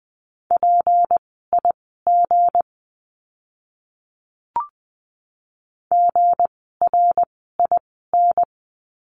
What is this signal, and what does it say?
Morse code, heard as a keyed tone, about 20 wpm, spaced as for 15 wpm, 700 Hz. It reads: PIG GRIN
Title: 160 Lesson 26f - Letter g - Words 20wpm